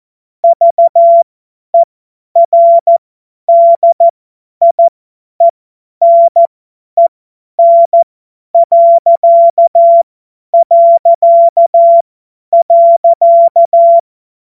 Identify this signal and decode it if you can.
Morse code, heard as a keyed tone, about 14 wpm, spaced as for 10 wpm, 680 Hz. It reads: VERDIENEN...
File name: Morsekurs / Witze / Witz 08. August 2023 50 F70